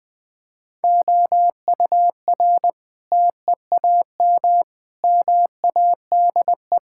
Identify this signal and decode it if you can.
Morse code, heard as a keyed tone, about 20 wpm, 700 Hz. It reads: OUR TEAM MADE
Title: Sentences with 7-Word Phrases - 20wpm